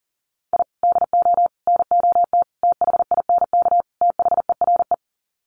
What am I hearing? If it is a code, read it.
Morse code, heard as a keyed tone, about 37 wpm, 700 Hz. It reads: I DO NOT THINK THERE
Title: Sets of 5 Words 40wpm